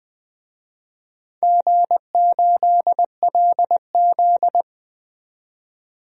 Morse code, heard as a keyed tone, about 20 wpm, 700 Hz. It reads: G8LZ
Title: DXCC-100 Call Signs - 4 Characters - 20wpm